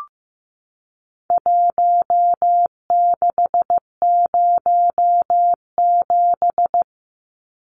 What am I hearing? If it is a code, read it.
Morse code, heard as a keyed tone, about 15 wpm, 700 Hz. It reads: 1607
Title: Numbers - Contest Style - 4-Digit 15wpm